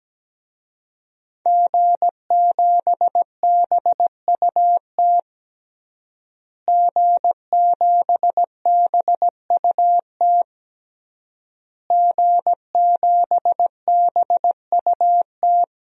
Morse code, heard as a keyed tone, about 17 wpm, 700 Hz. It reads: G7BUT G7BUT G7BUT
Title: DXCC-100 Call Signs - 5 Characters - Repeated 3x - 17wpm